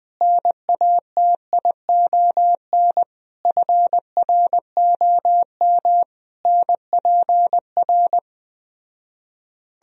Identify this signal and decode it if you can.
Morse code, heard as a keyed tone, about 20 wpm, 700 Hz. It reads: NATION FROM NPR